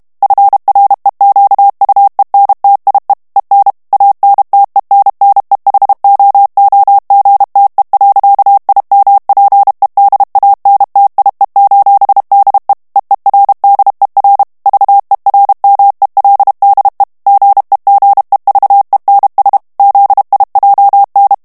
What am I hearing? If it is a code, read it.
Morse code, heard as a keyed tone, about 32 wpm, 800 Hz. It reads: FREQUENTIE EN ANTENNEHOOGTE. IMPEDANTIE: DE EERDER VERMELDE GEGEVENS ZIJN
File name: HOLLANDAIS texte clair 1 vitesse 30 mots minute